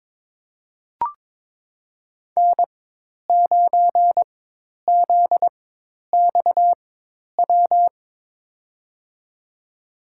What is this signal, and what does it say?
Morse code, heard as a keyed tone, about 22 wpm, spaced as for 10 wpm, 700 Hz. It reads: N9ZXW
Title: USA Call Signs - 5 Characters - 4x Character Spacing - 22wpm